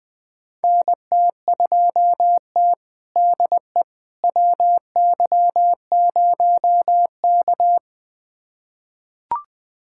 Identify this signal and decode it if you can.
Morse code, heard as a keyed tone, about 21 wpm, 700 Hz. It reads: NT2T DE WY0K
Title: USA Callsign DE Callsign - 4 Characters - 20wpm